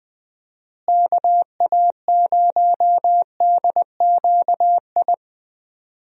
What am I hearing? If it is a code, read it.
Morse code, heard as a keyed tone, about 20 wpm, 700 Hz. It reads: KA0DQI
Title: USA Call Signs - 6 Characters - Repeated 2x - 20wpm